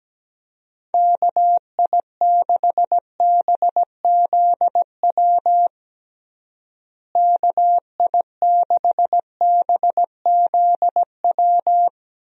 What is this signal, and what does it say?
Morse code, heard as a keyed tone, about 17 wpm, 700 Hz. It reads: KI6BZW KI6BZW
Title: USA Call Signs - 6 Characters - Repeated 3x - 17wpm